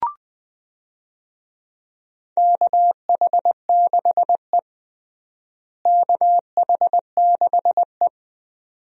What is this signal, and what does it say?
Morse code, heard as a keyed tone, about 21 wpm, 700 Hz. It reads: KH6E KH6E
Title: DXCC-100 Call Signs - 4 Characters - Repeated 2x - 20wpm